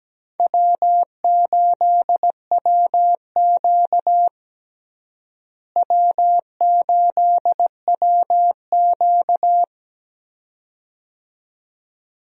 Morse code, heard as a keyed tone, about 17 wpm, 700 Hz. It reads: W8WQ W8WQ
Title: USA Call Signs - 4 Characters - Repeated 2x - 17wpm